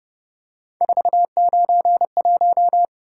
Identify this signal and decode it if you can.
Morse code, heard as a keyed tone, about 29 wpm, 700 Hz. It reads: 491
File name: Speed-Racing - Three Letter-Number Combinations - 25wpm